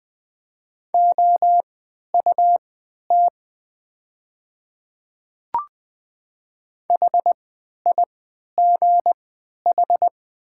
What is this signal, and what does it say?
Morse code, heard as a keyed tone, about 20 wpm, spaced as for 11 wpm, 700 Hz. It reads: OUT HIGH
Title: 3x Character Spacing - Top 300 Words - 20wpm